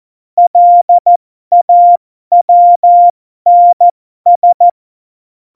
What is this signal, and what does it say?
Morse code, heard as a keyed tone, about 14 wpm, spaced as for 12 wpm, 700 Hz. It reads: LAWNS
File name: Morsekurs / Witze